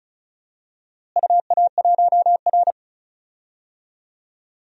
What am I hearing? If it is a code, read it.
Morse code, heard as a keyed tone, about 36 wpm, 700 Hz. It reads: UA1R
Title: DXCC-100 Call Signs - 4 Characters - Repeated 2x - 35wpm